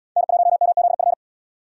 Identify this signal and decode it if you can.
Morse code, heard as a keyed tone, about 71 wpm, 700 Hz. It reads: N1MZF